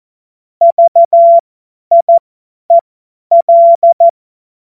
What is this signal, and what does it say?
Morse code, heard as a keyed tone, about 14 wpm, spaced as for 10 wpm, 680 Hz. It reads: VIEL
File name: Morsekurs / Witze / Witz 02. Mai 2023 50 F70